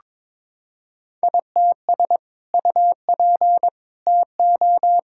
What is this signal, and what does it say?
Morse code, heard as a keyed tone, about 22 wpm, 700 Hz. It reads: ITS UP TO
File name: practice / 22wpm / Sentences with 2-Word Phrases - 22wpm